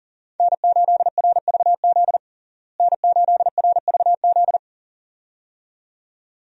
Morse code, heard as a keyed tone, about 37 wpm, 700 Hz. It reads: N8RVZ N8RVZ
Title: USA Call Signs - 5 Characters - Repeated 2x - 40wpm